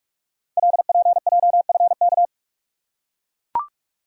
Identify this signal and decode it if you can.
Morse code, heard as a keyed tone, about 44 wpm, 700 Hz. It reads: LPJFK
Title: Code Groups - 5 Letters - 45wpm